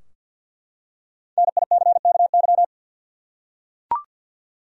Morse code, heard as a keyed tone, about 49 wpm, 700 Hz. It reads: NICKY
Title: USA Common Names - Max 5 Letters - 50wpm